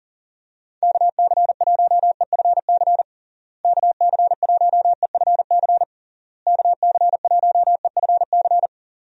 Morse code, heard as a keyed tone, about 40 wpm, 700 Hz. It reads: KC1EFC KC1EFC KC1EFC
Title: USA Call Signs - 6 Characters - Repeated 3x - 40wpm